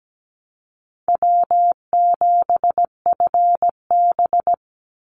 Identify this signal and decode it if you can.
Morse code, heard as a keyed tone, about 17 wpm, 700 Hz. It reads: W7FB
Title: Call Signs - 17wpm